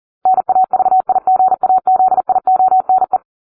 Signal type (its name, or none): Morse code